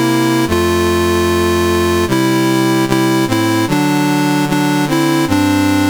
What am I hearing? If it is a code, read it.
There is no code.